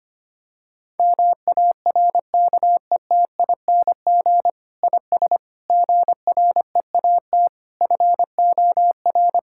Morse code, heard as a keyed tone, about 25 wpm, 700 Hz. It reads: MARKETING IS GREAT FOR
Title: Sentences from Top 1000 Words - Review Part 5 - 25wpm